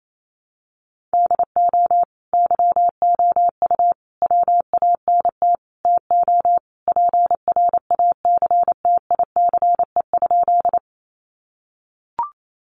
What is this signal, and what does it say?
Morse code, heard as a keyed tone, about 28 wpm, 700 Hz. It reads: DO YOU WANT TO PRACTICE?
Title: Sentences from Top 700 Words 28wpm